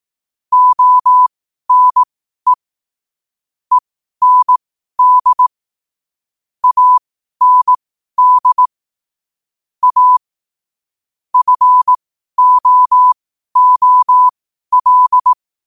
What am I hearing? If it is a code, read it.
Morse code, heard as a keyed tone, about 18 wpm, spaced as for 12 wpm, 1000 Hz. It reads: ONE END AND A FOOL
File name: qotdfiles / 12wpm